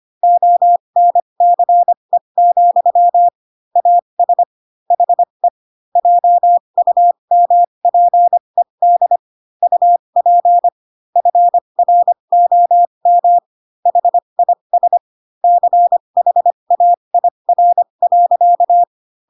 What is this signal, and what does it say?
Morse code, heard as a keyed tone, about 25 wpm, spaced as for 22 wpm, 700 Hz. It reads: ONCE, AS HE JUMPED UP FROM HIS CHAIR.